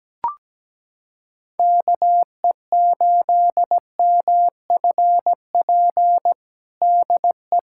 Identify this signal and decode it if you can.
Morse code, heard as a keyed tone, about 17 wpm, 700 Hz. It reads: KE8MFP DE
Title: USA Callsign DE Callsign - 6 Characters - 17wpm